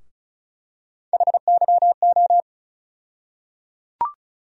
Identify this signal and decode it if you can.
Morse code, heard as a keyed tone, about 36 wpm, 700 Hz. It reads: HYO
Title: USA Call Sign Postfixes - 3 Letters 35wpm